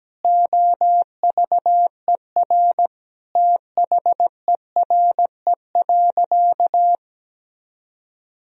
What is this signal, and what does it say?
Morse code, heard as a keyed tone, about 17 wpm, 700 Hz. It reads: OVER THERE.
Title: Sentences from Top 1000 Words - Review Part 3 - 17wpm